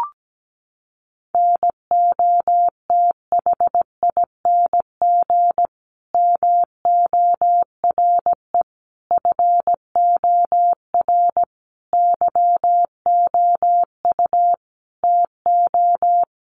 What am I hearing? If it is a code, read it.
Morse code, heard as a keyed tone, about 17 wpm, 700 Hz. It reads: NOTHING MORE FOR YOU TO
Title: Sentences from Top 1100 Words 17wpm